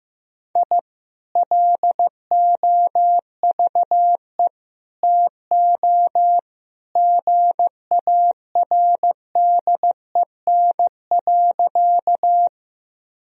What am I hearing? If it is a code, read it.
Morse code, heard as a keyed tone, about 15 wpm, 700 Hz. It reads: I LOVE TO GARDEN.
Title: Sentences from Top 1000 Words - Review Part 4 - 15wpm